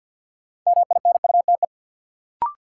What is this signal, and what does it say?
Morse code, heard as a keyed tone, about 51 wpm, 700 Hz. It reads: MINUTE
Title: Sentences from Top 500 Words - New Word Drill - 50wpm